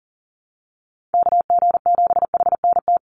Morse code, heard as a keyed tone, about 40 wpm, 700 Hz. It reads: KG7HNT
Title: USA Call Signs 40wpm